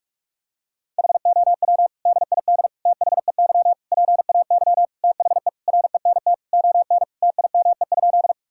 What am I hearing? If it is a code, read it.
Morse code, heard as a keyed tone, about 46 wpm, 700 Hz. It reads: HOW DID THEY PAY THE RENT ON TIME?
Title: Sentences from Top 1000 Words - Review Part 5 - 45wpm